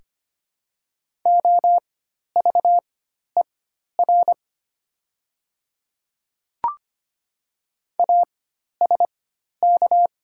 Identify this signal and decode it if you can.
Morse code, heard as a keyed tone, about 25 wpm, spaced as for 12 wpm, 700 Hz. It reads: OVER ASK
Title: 4x Character Spacing - Top 200 Words - 25wpm